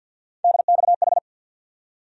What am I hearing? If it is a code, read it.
Morse code, heard as a keyed tone, about 50 wpm, 700 Hz. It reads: DXH
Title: USA Call Sign Postfixes - 3 Letters 50wpm